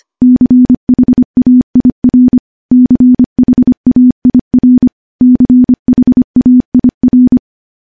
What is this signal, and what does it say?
Morse code, heard as a keyed tone, about 25 wpm, 250 Hz. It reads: CHAIR CHAIR CHAIR
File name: morse / morse-training-25WPM